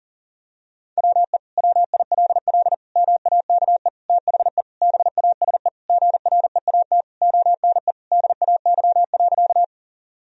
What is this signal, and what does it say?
Morse code, heard as a keyed tone, about 40 wpm, 700 Hz. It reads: WE WILL MAKE THE BASE GREAT ONE DAY.
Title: Sentences from Top 1000 Words - Review Part 1 - 40wpm